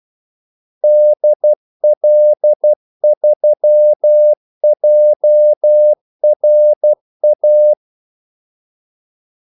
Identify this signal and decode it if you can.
Morse code, heard as a keyed tone, about 12 wpm, 600 Hz. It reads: DL3JRA
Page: 2 ekstra ordmellemrum mellem kaldesignalerne